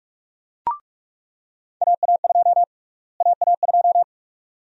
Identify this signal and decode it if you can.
Morse code, heard as a keyed tone, about 46 wpm, 700 Hz. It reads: AA2 AA2
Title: USA Call Sign Prefixes - Repeated 2x - 45wpm